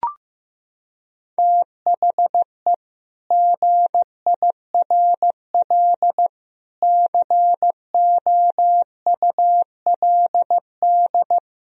Morse code, heard as a keyed tone, about 15 wpm, 700 Hz. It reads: THE GIRL COULD
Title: Sentences from Top 1000 Words - Review Part 4 - 15wpm